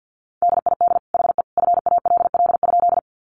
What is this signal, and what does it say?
Morse code, heard as a keyed tone, about 50 wpm, 700 Hz. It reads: DID HE FALL?
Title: Sentences from Top 500 Words 50wpm